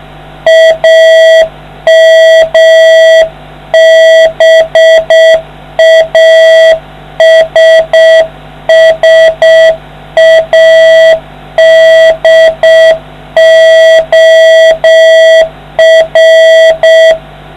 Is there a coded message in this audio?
Yes: Morse code from 0.5 to 17.1 s, heard as a keyed tone, about 6 wpm, 650 Hz, with noise. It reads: AMBASSADOR